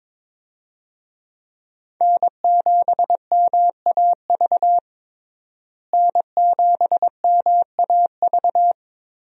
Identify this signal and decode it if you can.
Morse code, heard as a keyed tone, about 22 wpm, 700 Hz. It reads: N7MAV N7MAV
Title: USA Call Signs - 5 Characters - Repeated 3x - 22wpm